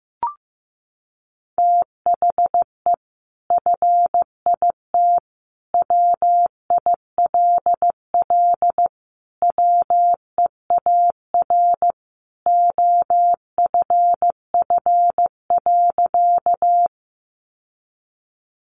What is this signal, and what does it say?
Morse code, heard as a keyed tone, about 15 wpm, 700 Hz. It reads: THE FIT WILL WEAR OFF.
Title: Sentences from Top 1500 Words 15wpm